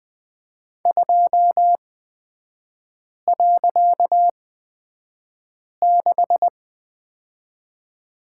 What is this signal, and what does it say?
Morse code, heard as a keyed tone, about 20 wpm, spaced as for 5 wpm, 700 Hz. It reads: 2.6
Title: 255 Lesson 41e - Period - Three Characters 20wpm